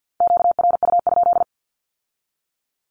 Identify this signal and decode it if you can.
Morse code, heard as a keyed tone, about 50 wpm, 700 Hz. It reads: QRU?
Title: Q-Codes with Spoken Meaning - 50wpm